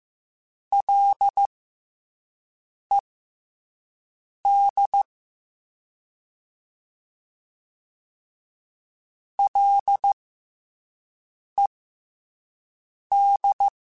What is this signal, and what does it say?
Morse code, heard as a keyed tone, about 15 wpm, spaced as for 5 wpm, 800 Hz. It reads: LED LED
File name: F and L / F and L mp3 / F and L Prefixes 3x2 Twice @5